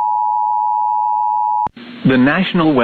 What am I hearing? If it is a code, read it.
Emergency Alert System (SAME) tones (the 853 and 960 Hz attention tone), with speech. It is heard from 0.0 to 1.7 s.